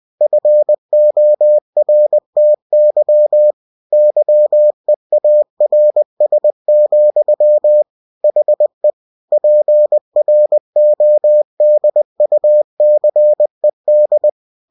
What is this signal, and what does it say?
Morse code, heard as a keyed tone, about 20 wpm, 600 Hz. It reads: FORTY YEARS, HE PRODUCED
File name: cw-books / winnie-the-pooh / winnie-20wpm-page-0122